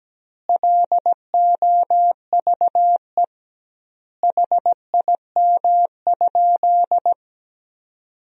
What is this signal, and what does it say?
Morse code, heard as a keyed tone, about 17 wpm, 700 Hz. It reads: LOVE HIM?
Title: 2x Word-Spacing - Sentences from Top 400 Words - 17wpm